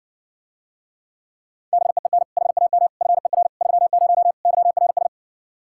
Morse code, heard as a keyed tone, about 67 wpm, 700 Hz. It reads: BEEN HAM FER 30 YRS